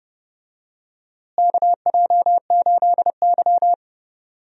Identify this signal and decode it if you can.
Morse code, heard as a keyed tone, about 30 wpm, 700 Hz. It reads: KJ8Y
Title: USA Call Signs - 4 Characters - Repeated 2x - 30wpm